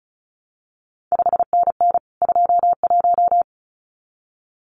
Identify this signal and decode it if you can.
Morse code, heard as a keyed tone, about 36 wpm, 700 Hz. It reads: 5NN 21
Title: CQ WW DX Contest Exchange Encore - 35wpm